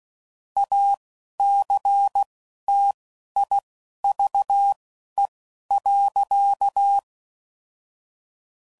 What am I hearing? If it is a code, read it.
Morse code, heard as a keyed tone, about 16 wpm, spaced as for 11 wpm, 800 Hz. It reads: ACTIVE.
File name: CW Code Groups @10WPM / Technical Subjects / OR Gate